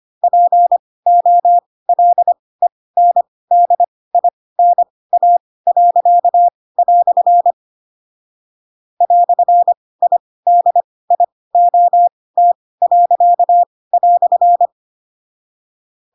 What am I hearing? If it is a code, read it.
Morse code, heard as a keyed tone, about 25 wpm, spaced as for 18 wpm, 700 Hz. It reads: POLENDINA." "IDIOT."
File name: Pinocchio / Pinocchio ch2 12 25 ik1hge